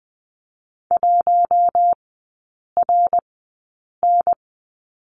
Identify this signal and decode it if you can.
Morse code, heard as a keyed tone, about 20 wpm, spaced as for 8 wpm, 700 Hz. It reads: 1RN
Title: 133 Lesson 22e - Question Mark - Three Characters 20wpm